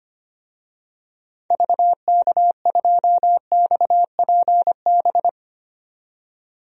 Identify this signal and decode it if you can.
Morse code, heard as a keyed tone, about 25 wpm, 700 Hz. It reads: VK2XPB
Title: DXCC-100 Call Signs - 6 Characters - 25wpm